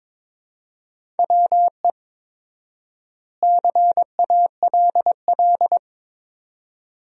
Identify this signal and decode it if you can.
Morse code, heard as a keyed tone, about 22 wpm, 700 Hz. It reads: WE CALL
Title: 4x Word-Spacing - Sentences from Top 400 Words - 22wpm